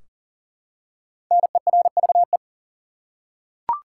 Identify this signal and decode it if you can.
Morse code, heard as a keyed tone, about 40 wpm, 700 Hz. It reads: NERVE
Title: Five Letter Words - 40wpm